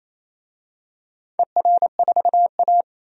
Morse code, heard as a keyed tone, about 28 wpm, 700 Hz. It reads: ER4A
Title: DXCC-100 Call Signs - 4 Characters - 28wpm